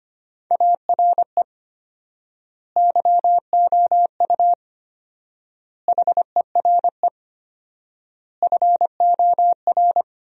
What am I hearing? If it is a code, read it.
Morse code, heard as a keyed tone, about 25 wpm, 700 Hz. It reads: ARE YOU HERE FOR
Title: 4x Word-Spacing - Sentences from Top 200 Words - 25wpm